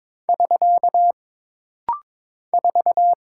Morse code, heard as a keyed tone, about 22 wpm, 700 Hz. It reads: <SK> 4
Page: This rapid-fire timing is geared to push your ICR (Instant Character Recognition) skills.